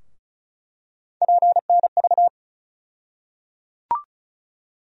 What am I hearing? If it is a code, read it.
Morse code, heard as a keyed tone, about 36 wpm, 700 Hz. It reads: PNV
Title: USA Call Sign Postfixes - 3 Letters 35wpm